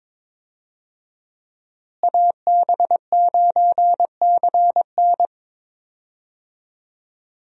Morse code, heard as a keyed tone, about 22 wpm, 700 Hz. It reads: AB9CN
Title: USA Call Signs - 5 Characters - Repeated 3x - 22wpm